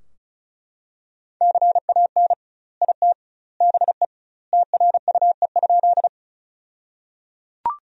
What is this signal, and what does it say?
Morse code, heard as a keyed tone, about 34 wpm, 700 Hz. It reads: CAN IT BE TRUE?
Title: 2x Word-Spacing - Sentences from Top 400 Words - 35wpm